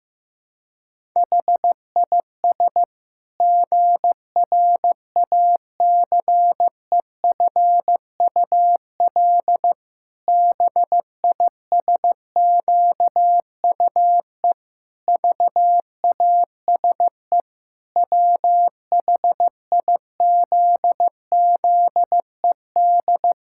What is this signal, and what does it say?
Morse code, heard as a keyed tone, about 15 wpm, 700 Hz. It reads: HIS GRACEFUL BISQUE VASE WHIZZED
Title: Pangram Challenge v2 - 15wpm